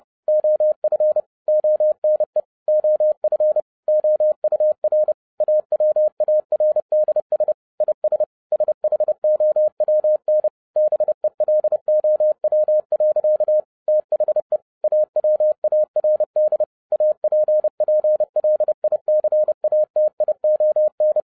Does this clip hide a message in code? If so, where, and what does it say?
Yes: Morse code from 0.3 to 21.2 s, heard as a keyed tone, about 30 wpm, 610 Hz. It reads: OF ONE OF OUR AWARDS IS SHOWN BELOW. THE AWARD APPLICATION